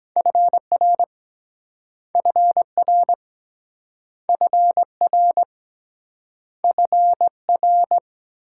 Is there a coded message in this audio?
Yes: Morse code from 0.2 to 8.0 s, heard as a keyed tone, about 23 wpm, 700 Hz. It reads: FR FR FR FR